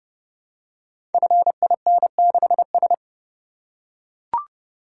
Morse code, heard as a keyed tone, about 28 wpm, 700 Hz. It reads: FIN6S
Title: Code Groups - 5 Letter:Numbers - 30wpm